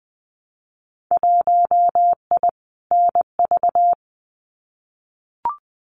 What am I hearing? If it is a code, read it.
Morse code, heard as a keyed tone, about 19 wpm, 700 Hz. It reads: 1I NV
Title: Winter Field Day Exchanges Encore 20wpm